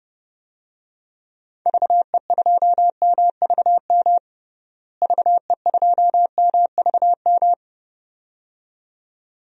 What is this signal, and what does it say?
Morse code, heard as a keyed tone, about 30 wpm, 700 Hz. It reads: VE2MVM VE2MVM
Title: DXCC-100 Call Signs - 6 Characters - Repeated 2x - 30wpm